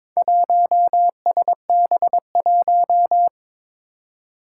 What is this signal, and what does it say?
Morse code, heard as a keyed tone, about 22 wpm, 700 Hz. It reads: 1SB1
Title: Four Challenging Letter-Number Combinations 22wpm